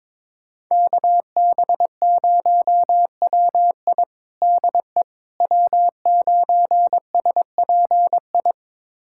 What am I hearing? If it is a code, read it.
Morse code, heard as a keyed tone, about 22 wpm, 700 Hz. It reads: KB0WI DE W9SPI
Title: USA Callsign DE Callsign - 5 Characters - 22wpm